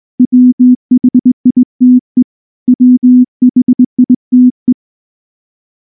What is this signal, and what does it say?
Morse code, heard as a keyed tone, about 19 wpm, 250 Hz. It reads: WHITE WHITE
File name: morse / morse-training-20WPM-5